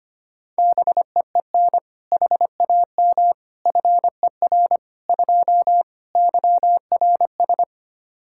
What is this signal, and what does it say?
Morse code, heard as a keyed tone, about 25 wpm, 700 Hz. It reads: BEEN HAM FER 2 YRS
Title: K1USN SST QSO Practice 25wpm